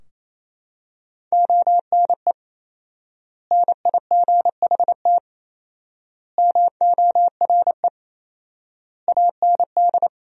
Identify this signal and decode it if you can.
Morse code, heard as a keyed tone, about 28 wpm, 700 Hz. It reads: ONE NIGHT MORE AND
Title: 4x Word-Spacing - Sentences from Top 300 Words - 28wpm